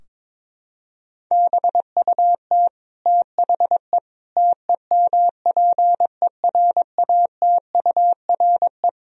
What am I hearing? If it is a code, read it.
Morse code, heard as a keyed tone, about 22 wpm, 700 Hz. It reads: BUT THE TEMPERATURE
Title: Sentences from Top 1000 Words - Review Part 4 - 22wpm